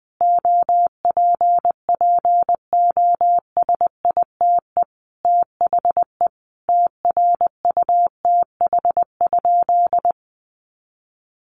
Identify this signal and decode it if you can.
Morse code, heard as a keyed tone, about 20 wpm, 700 Hz. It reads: OPPOSITE THE TRUTH?
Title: Sentences from Top 1300 Words 20wpm